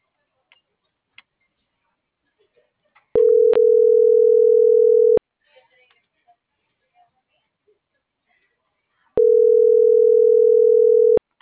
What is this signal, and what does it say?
Ringback tone (North America), 440 + 480 Hz.